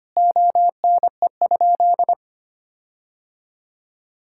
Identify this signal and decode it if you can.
Morse code, heard as a keyed tone, about 25 wpm, 700 Hz. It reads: ONE?
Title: Sentences from Top 1600 Words 25wpm